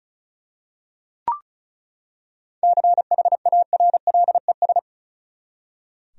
Morse code, heard as a keyed tone, about 34 wpm, 700 Hz. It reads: CHARLES
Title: USA Common Names - 35wpm